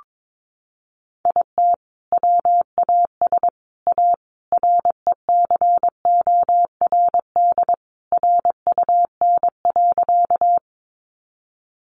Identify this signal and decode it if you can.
Morse code, heard as a keyed tone, about 22 wpm, 700 Hz. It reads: IT WAS A RECORD RUN.
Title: Sentences from Top 500 Words 22wpm